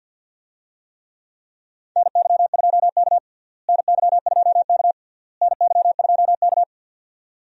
Morse code, heard as a keyed tone, about 47 wpm, 700 Hz. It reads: NY2K NY2K NY2K
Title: USA Call Signs - 4 Characters - Repeated 3x - 50wpm